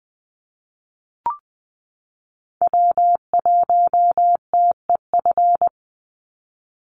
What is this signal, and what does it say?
Morse code, heard as a keyed tone, about 20 wpm, 700 Hz. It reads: W1TEF
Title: Call Signs Repeated 2x - 20wpm